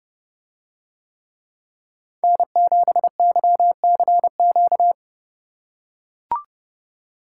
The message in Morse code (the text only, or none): N7YCQ